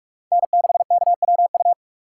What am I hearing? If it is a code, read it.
Morse code, heard as a keyed tone, about 41 wpm, 700 Hz. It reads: N6KWU
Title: USA Call Signs - 5 Characters - 2x Character Spacing - 45wpm